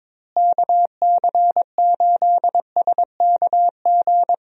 Morse code, heard as a keyed tone, about 22 wpm, 700 Hz. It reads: KC8SKG
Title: USA Call Signs - 6 Characters - Repeated 2x - 22wpm